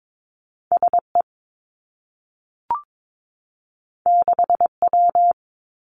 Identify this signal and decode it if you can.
Morse code, heard as a keyed tone, about 22 wpm, 700 Hz. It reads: S E 6 W
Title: Two Challenging Letter-Number Combinations 22wpm